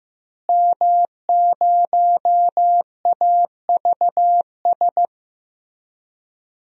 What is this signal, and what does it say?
Morse code, heard as a keyed tone, about 15 wpm, 700 Hz. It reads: M0AVS
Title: DXCC-100 Call Signs - 5 Characters - Repeated 3x - 15wpm